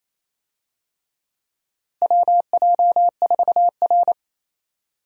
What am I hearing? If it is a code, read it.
Morse code, heard as a keyed tone, about 28 wpm, 700 Hz. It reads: WJ4R
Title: USA Call Signs - 4 Characters - Repeated 3x - 28wpm